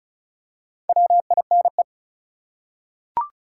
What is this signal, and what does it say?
Morse code, heard as a keyed tone, about 34 wpm, 700 Hz. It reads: WINE
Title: Four Letter Words - 35wpm